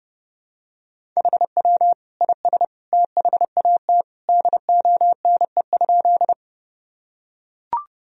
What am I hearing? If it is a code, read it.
Morse code, heard as a keyed tone, about 30 wpm, 700 Hz. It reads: HW IS THAT DONE?
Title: Sentences from Top 500 Words and Abbreviations 30wpm